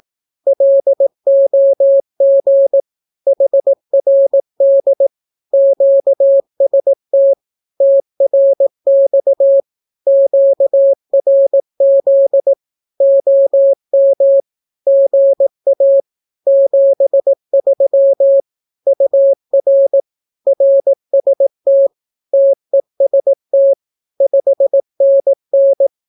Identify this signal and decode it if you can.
Morse code, heard as a keyed tone, about 18 wpm, 550 Hz. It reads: LOG HRD QST TRX QRZ OM GA 73 UR RST TEST 5NN
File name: naukapodstaw / skroty / skroty